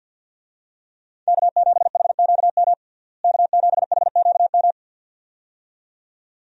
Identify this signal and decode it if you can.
Morse code, heard as a keyed tone, about 50 wpm, 700 Hz. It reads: K7HQM K7HQM
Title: USA Call Signs - 5 Characters - Repeated 2x - 50wpm